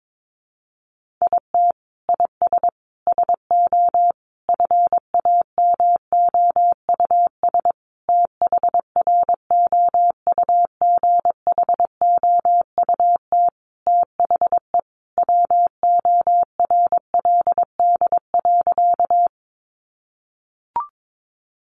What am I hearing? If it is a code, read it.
Morse code, heard as a keyed tone, about 22 wpm, 700 Hz. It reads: IT IS SO FAMOUS THROUGHOUT THE WORLD.
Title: Sentences from Top 1500 Words 22wpm